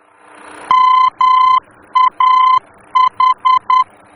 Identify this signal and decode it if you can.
Morse code, heard as a keyed tone, about 10 wpm, 1020 Hz. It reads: MAH
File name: germany / mx / MAH-D-VOR